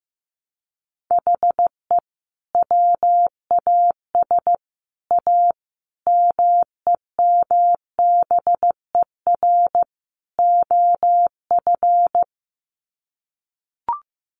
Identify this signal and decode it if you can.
Morse code, heard as a keyed tone, about 15 wpm, 700 Hz. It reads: HE WAS A MEMBER OF
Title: Sets of 5 Words Encore2 15wpm